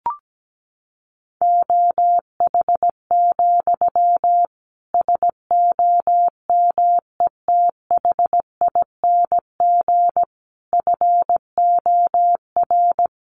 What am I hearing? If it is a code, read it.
Morse code, heard as a keyed tone, about 17 wpm, 700 Hz. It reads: OH, SOMETHING FOR
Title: Sentences from Top 1500 Words 17wpm